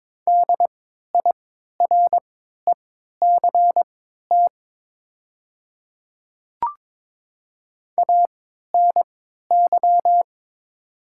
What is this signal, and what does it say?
Morse code, heard as a keyed tone, about 22 wpm, spaced as for 12 wpm, 690 Hz. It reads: DIRECT ANY
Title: 3x Character Spacing - Top 500 Words - 22wpm